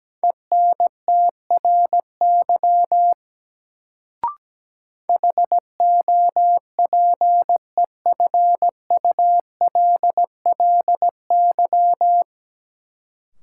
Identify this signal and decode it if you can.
Morse code, heard as a keyed tone, about 17 wpm, 700 Hz. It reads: ENTRY HOPEFULLY
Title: Sentences from Top 1500 Words - New Word Drill Alt - 17wpm